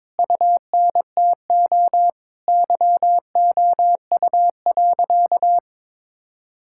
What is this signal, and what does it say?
Morse code, heard as a keyed tone, about 22 wpm, 700 Hz. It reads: UNTO YOU.
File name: Sentences with 9-Word Phrases - 22wpm